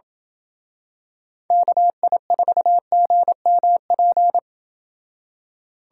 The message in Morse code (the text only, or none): KI4GMP